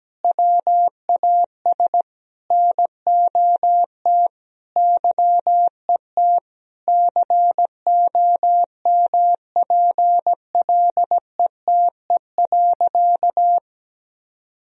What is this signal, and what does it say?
Morse code, heard as a keyed tone, about 17 wpm, 700 Hz. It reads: WAS NOT YET COMPLETE.
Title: Sentences from Top 1000 Words - Review Part 2 - 17wpm